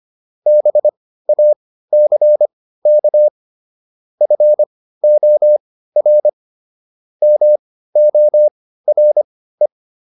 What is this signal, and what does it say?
Morse code, heard as a keyed tone, about 25 wpm, spaced as for 15 wpm, 600 Hz. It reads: BACK FOR MORE